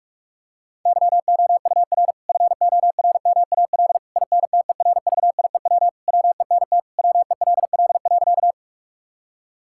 Morse code, heard as a keyed tone, about 46 wpm, 700 Hz. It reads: YOUR FORMAL INTERVIEW WENT WELL.
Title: Sentences from Top 1600 Words 45wpm